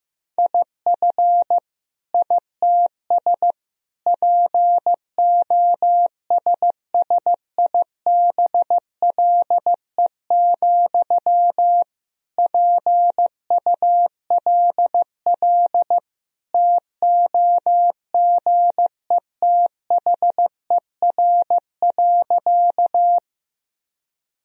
Morse code, heard as a keyed tone, about 15 wpm, 700 Hz. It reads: IF ITS POSSIBLE, PULL TOGETHER.
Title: Sentences from Top 1000 Words - Review Part 2 - 15wpm